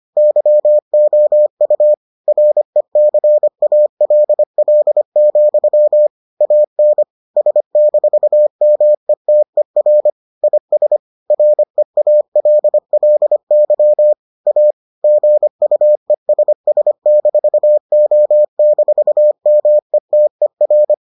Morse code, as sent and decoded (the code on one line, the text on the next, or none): -.-- --- ..- / .-. -.-. .- .-.. .-.. --..-- / .- -. / ... -....- -- . - . .-. / .. ... / .-. .- .-.. .-.. -.-- / .- / --. ..- . ... ... -....- --- -....- -- . - . .-.
YOU RECALL, AN S-METER IS REALLY A GUESS-O-METER